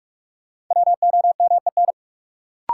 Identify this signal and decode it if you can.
Morse code, heard as a keyed tone, about 46 wpm, 700 Hz. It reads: WOMEN
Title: Sentences from Top 1300 Words - New Word Drill Alt - 45wpm